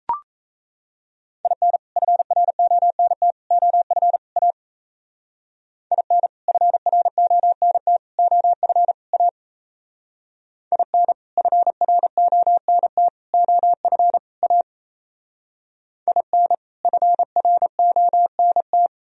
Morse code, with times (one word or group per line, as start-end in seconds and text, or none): 1.4-1.8 IN
2.0-3.3 FRONT
3.5-4.2 OF
4.4-4.5 A
5.9-6.3 IN
6.5-8.0 FRONT
8.2-8.9 OF
9.1-9.3 A
10.7-11.1 IN
11.4-13.1 FRONT
13.3-14.2 OF
14.4-14.6 A
16.1-16.5 IN
16.8-18.9 FRONT